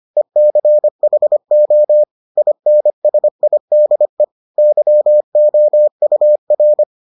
Morse code, heard as a keyed tone, about 25 wpm, 600 Hz. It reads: ECHO INSIDE YOUR